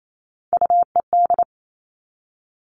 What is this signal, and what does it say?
Morse code, heard as a keyed tone, about 28 wpm, 700 Hz. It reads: UED
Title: All Cut Numbers - 3-Digit 28wpm